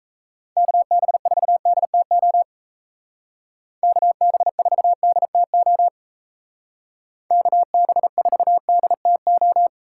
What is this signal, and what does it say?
Morse code, heard as a keyed tone, about 33 wpm, 700 Hz. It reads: KB4DTO KB4DTO KB4DTO
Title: Speed Racing - USA 2x3 Call Signs 28wpm